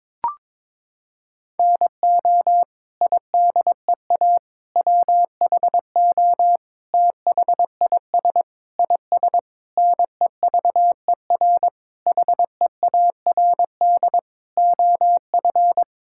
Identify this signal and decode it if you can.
Morse code, heard as a keyed tone, about 22 wpm, 700 Hz. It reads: NO IDEA WHO THIS IS NEVER HEARD OF
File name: Sets of 9 Words - 22wpm